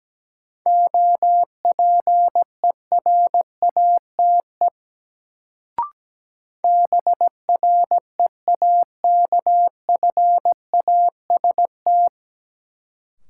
Morse code, heard as a keyed tone, about 17 wpm, 700 Hz. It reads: OPERATE BREAKFAST
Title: Sentences from Top 1500 Words - New Word Drill Alt - 17wpm